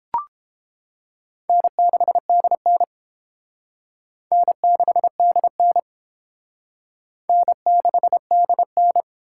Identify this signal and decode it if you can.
Morse code, heard as a keyed tone, about 31 wpm, 700 Hz. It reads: N6DN N6DN N6DN